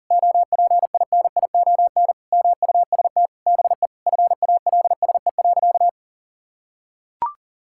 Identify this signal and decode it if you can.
Morse code, heard as a keyed tone, about 37 wpm, 700 Hz. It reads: OPINION MUST BE FALSE.
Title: Sentences from Top 1600 Words 40wpm